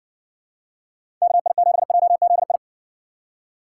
Morse code, heard as a keyed tone, about 60 wpm, 700 Hz. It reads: KI7JZI